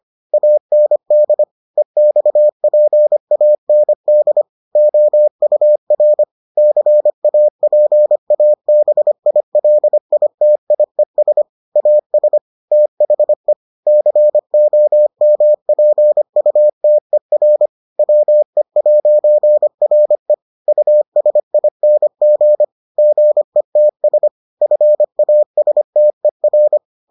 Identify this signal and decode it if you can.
Morse code, heard as a keyed tone, about 25 wpm, 600 Hz. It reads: AND EXPAND OUR CAPABILITIES AS THE COMPUTER WE'RE USING GETS FASTER